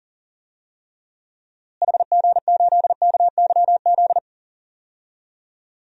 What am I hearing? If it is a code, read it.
Morse code, heard as a keyed tone, about 37 wpm, 700 Hz. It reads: HG8KYZ
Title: DXCC-100 Call Signs - 6 Characters - Repeated 2x - 40wpm